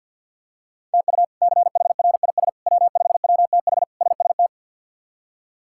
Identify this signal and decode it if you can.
Morse code, heard as a keyed tone, about 46 wpm, 700 Hz. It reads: TU CHRIS W5WTH SST
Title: K1USN SST QSO Practice - 50wpm